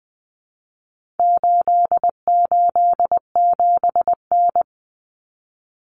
Morse code, heard as a keyed tone, about 20 wpm, 700 Hz. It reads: 887N
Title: Common Cut Numbers - 4-Digit 20wpm